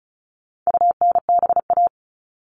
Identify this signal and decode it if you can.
Morse code, heard as a keyed tone, about 34 wpm, 700 Hz. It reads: UNBA
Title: All Cut Numbers - 4-Digit 35wpm